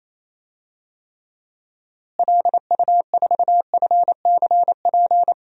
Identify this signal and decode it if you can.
Morse code, heard as a keyed tone, about 28 wpm, 700 Hz. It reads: LU4FCP
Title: DXCC-100 Call Signs - 6 Characters - Repeated 2x - 28wpm